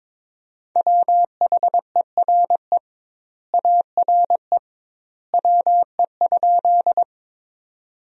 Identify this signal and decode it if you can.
Morse code, heard as a keyed tone, about 22 wpm, 700 Hz. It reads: WHERE ARE WE?
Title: 2x Word-Spacing - Sentences from Top 200 Words - 22wpm